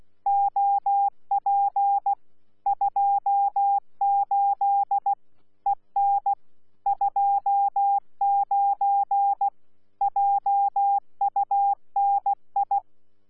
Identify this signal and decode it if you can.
Morse code, heard as a keyed tone, about 16 wpm, 800 Hz. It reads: OP 28 EN 29 JUNI
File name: Z PI4VRZ / PI4VRZ 14-jun-2025